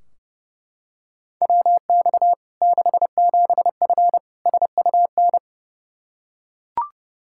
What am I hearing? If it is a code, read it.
Morse code, heard as a keyed tone, about 30 wpm, 700 Hz. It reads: WX 67F SUN
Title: K1USN SST QSO Practice 30wpm